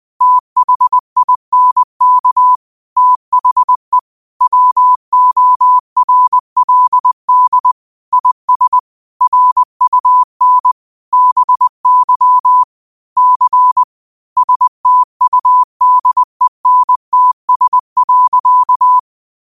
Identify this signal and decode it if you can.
Morse code, heard as a keyed tone, about 20 wpm, 1000 Hz. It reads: THINK THE WORLD IS RUN BY C STUDENTS.
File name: qotdfiles / 20wpm